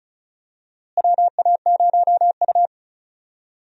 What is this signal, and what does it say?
Morse code, heard as a keyed tone, about 36 wpm, 700 Hz. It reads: WA0U
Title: USA Call Signs - 4 Characters - Repeated 3x - 35wpm